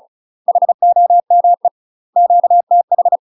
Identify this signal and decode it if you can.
Morse code, heard as a keyed tone, about 36 wpm, 700 Hz. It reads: HOME QTH
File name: UK MASTER QSOs TRAINING / 35WPM / UK MASTER QSOs 0038